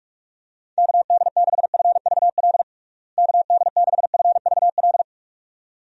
Morse code, heard as a keyed tone, about 46 wpm, 700 Hz. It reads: KD6FVL KD6FVL
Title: USA Call Signs - 6 Characters - Repeated 2x - 45wpm